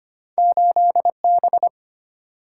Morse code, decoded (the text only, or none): 8 B